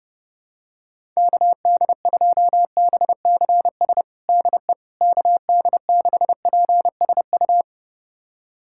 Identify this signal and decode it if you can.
Morse code, heard as a keyed tone, about 30 wpm, 700 Hz. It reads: KD2BCS DE KD6PSU
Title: USA Callsign DE Callsign 30wpm